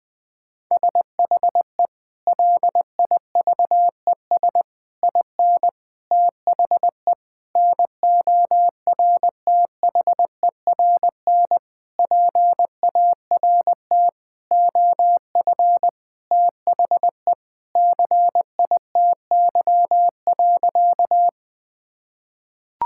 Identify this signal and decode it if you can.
Morse code, heard as a keyed tone, about 20 wpm, 700 Hz. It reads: SHE LIVES IN THE NORTHERN PART OF THE CITY.
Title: Sentences with 6-Word Phrases - Spoken Beginning - 20wpm